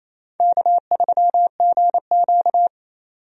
Morse code, heard as a keyed tone, about 28 wpm, 700 Hz. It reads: K3GQ
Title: USA Call Signs - 4 Characters - 28wpm